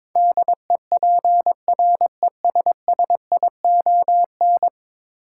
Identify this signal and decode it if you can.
Morse code, heard as a keyed tone, about 22 wpm, 700 Hz. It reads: DEPRESSION
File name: practice / 22wpm / 10 Letter Words - First 3 Letters Given - 22wpm